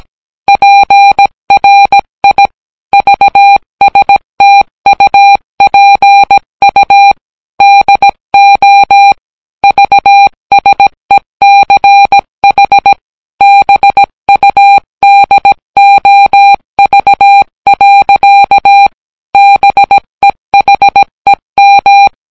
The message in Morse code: PRI VSTUPU DO VSECH BUDOV. BEHEM